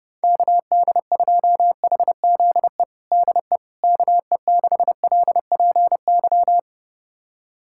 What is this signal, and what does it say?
Morse code, heard as a keyed tone, about 30 wpm, 700 Hz. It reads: KD2HZE DE KE6LPY
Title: USA Callsign DE Callsign - 6 Characters - 30wpm